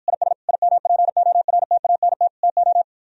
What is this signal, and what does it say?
Morse code, heard as a keyed tone, about 53 wpm, 700 Hz. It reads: IS IMPORTANT TO